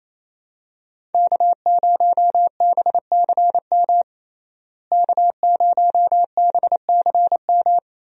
Morse code, heard as a keyed tone, about 28 wpm, 700 Hz. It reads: K0BCM K0BCM
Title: USA Call Signs - 5 Characters - Repeated 2x - 28wpm